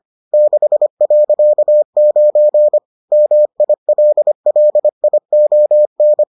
Morse code, heard as a keyed tone, about 25 wpm, 600 Hz. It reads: 6.9 MILLION